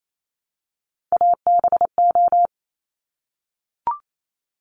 Morse code, heard as a keyed tone, about 28 wpm, 700 Hz. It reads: ABO
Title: Top Three Letter Combinations in Words 28wpm